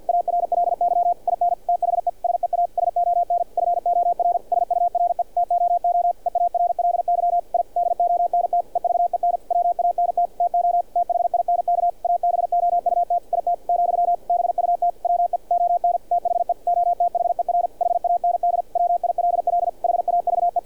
Morse code, heard as a keyed tone, about 48 wpm, 700 Hz, with radio noise. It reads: ARLY IN THE SEASON FOR SURE TOO EARLY I DONT EVEN WANT TO THINK ABOUT IT , BUT WE ON THE OTHER HAND WILL HAVE
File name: archived / CRSnet